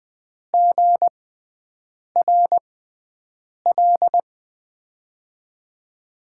Morse code, heard as a keyed tone, about 20 wpm, spaced as for 7 wpm, 700 Hz. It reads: GRL
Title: USA Call Signs - 6x Character Spacing - 20wpm